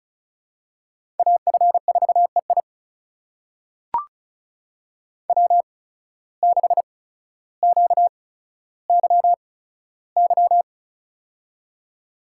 Morse code, heard as a keyed tone, about 36 wpm, spaced as for 15 wpm, 700 Hz. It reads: AF4EI W6QYY
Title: USA Call Signs - 5 Characters - 8x Character Spacing - 35wpm